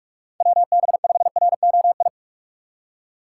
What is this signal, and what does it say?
Morse code, heard as a keyed tone, about 44 wpm, 700 Hz. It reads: WB5ROI